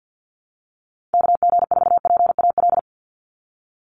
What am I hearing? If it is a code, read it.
Morse code, heard as a keyed tone, about 50 wpm, 700 Hz. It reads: KG4PAL
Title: USA Call Signs 50wpm